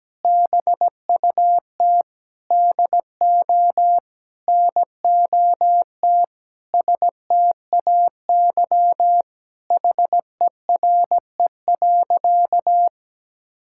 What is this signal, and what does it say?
Morse code, heard as a keyed tone, about 17 wpm, 700 Hz. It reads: BUT DO NOT STAY HERE.
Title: Sentences from Top 1600 Words 17wpm